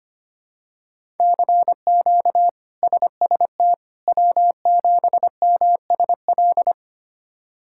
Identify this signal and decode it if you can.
Morse code, heard as a keyed tone, about 25 wpm, 700 Hz. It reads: CQ SST W7MSL
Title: K1USN SST QSO Practice - 25wpm